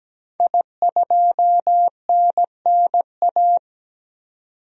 Morse code, heard as a keyed tone, about 17 wpm, 700 Hz. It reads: I2NNA
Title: DXCC-100 Call Signs - 5 Characters - 17wpm